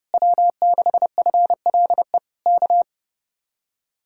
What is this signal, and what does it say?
Morse code, heard as a keyed tone, about 32 wpm, 700 Hz. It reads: W6FLE K